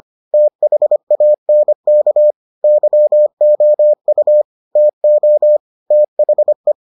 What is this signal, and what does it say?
Morse code, heard as a keyed tone, about 25 wpm, 600 Hz. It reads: THANK YOU TO THE